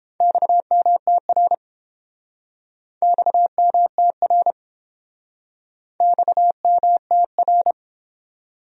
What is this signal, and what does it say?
Morse code, heard as a keyed tone, about 29 wpm, 700 Hz. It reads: XMTR XMTR XMTR